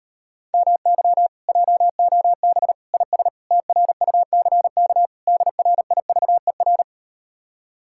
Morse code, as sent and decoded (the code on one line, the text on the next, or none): -- -.-- / .--- --- -... / .. ... / - .-. ..- -.-. -.- / -.. .-. .. ...- . .-.
MY JOB IS TRUCK DRIVER